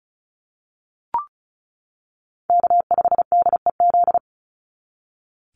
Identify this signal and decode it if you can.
Morse code, heard as a keyed tone, about 34 wpm, 700 Hz. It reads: K5DEZ
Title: NA SOTA Chaser Callsigns 35wpm